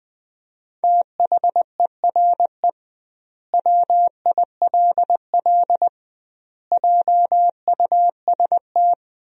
Morse code, heard as a keyed tone, about 20 wpm, 700 Hz. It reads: THERE WILL JUST
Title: 2x Word-Spacing - Sentences from Top 200 Words - 20wpm